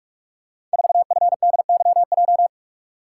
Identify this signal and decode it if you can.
Morse code, heard as a keyed tone, about 44 wpm, 700 Hz. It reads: 4FDYJ
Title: Code Groups - 5 Letter:Numbers - 45wpm